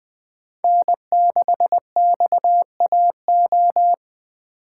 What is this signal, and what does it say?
Morse code, heard as a keyed tone, about 20 wpm, 700 Hz. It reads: N6XAO
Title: USA Call Signs - 5 Characters - 8x Character Spacing - 20wpm